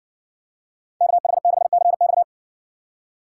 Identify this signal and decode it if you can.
Morse code, heard as a keyed tone, about 60 wpm, 700 Hz. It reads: KH6CX